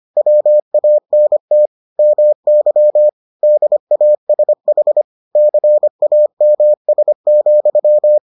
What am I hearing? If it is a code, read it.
Morse code, heard as a keyed tone, about 25 wpm, 600 Hz. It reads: WANT MY DASH CAMS,